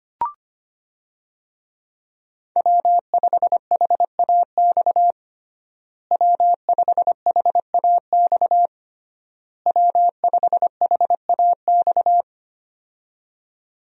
Morse code, heard as a keyed tone, about 25 wpm, 700 Hz. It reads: W5HAX W5HAX W5HAX
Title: USA Call Signs - 5 Characters - Repeated 3x - 25wpm